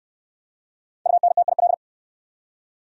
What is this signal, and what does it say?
Morse code, heard as a keyed tone, about 63 wpm, 700 Hz. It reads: UNTIL